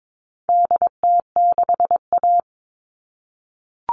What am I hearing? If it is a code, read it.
Morse code, heard as a keyed tone, about 22 wpm, 700 Hz. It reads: DT6A
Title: All Cut Numbers - 4-Digit 22wpm